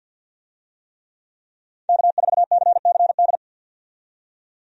Morse code, heard as a keyed tone, about 51 wpm, 700 Hz. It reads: K4CCD
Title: USA Call Signs - 5 Characters - 50wpm